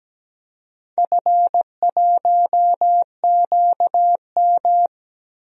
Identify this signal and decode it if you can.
Morse code, heard as a keyed tone, about 17 wpm, 700 Hz. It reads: F1QM
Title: DXCC-100 Call Signs - 4 Characters - Repeated 2x - 17wpm